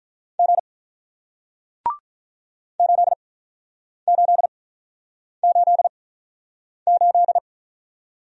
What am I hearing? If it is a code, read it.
Morse code, heard as a keyed tone, about 48 wpm, 700 Hz. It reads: G 8 8 8 8